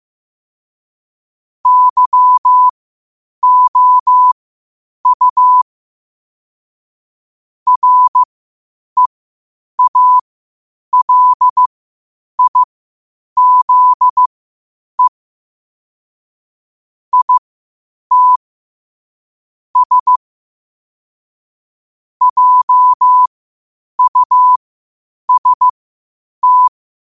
Morse code, heard as a keyed tone, about 15 wpm, spaced as for 8 wpm, 1000 Hz. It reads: YOU REALIZE IT S JUST
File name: qotdfiles / 8wpm